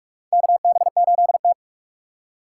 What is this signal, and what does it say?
Morse code, heard as a keyed tone, about 46 wpm, 690 Hz. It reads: KB8T
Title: USA Call Signs - 4 Characters - 3x Character Spacing - 45wpm